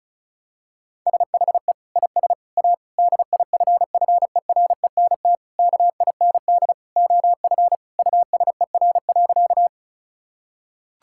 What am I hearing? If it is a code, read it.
Morse code, heard as a keyed tone, about 36 wpm, 700 Hz. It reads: SHE IS A DIFFERENT KIND OF USER.
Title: Sentences from Top 1000 Words - Review Part 5 - 35wpm